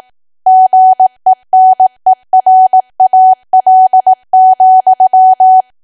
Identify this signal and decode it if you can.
Morse code, heard as a keyed tone, about 18 wpm, 750 Hz. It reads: GENERAL,
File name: cwa / W1AW-Level2 / 18WPM